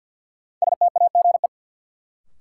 Morse code, heard as a keyed tone, about 53 wpm, 700 Hz. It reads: STAGE
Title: Sentences from Top 1200 Words - New Word Drill - 50wpm